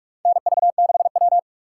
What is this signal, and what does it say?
Morse code, heard as a keyed tone, about 47 wpm, 700 Hz. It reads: NV6W
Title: USA Call Signs - 4 Characters - 4x Character Spacing - 45wpm